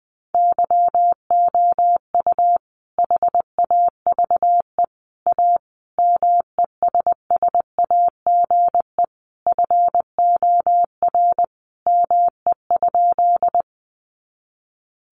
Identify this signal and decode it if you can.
Morse code, heard as a keyed tone, about 20 wpm, 700 Hz. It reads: YOU HAVE A MESSAGE FOR ME?
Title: Sentences from Top 1100 Words 20wpm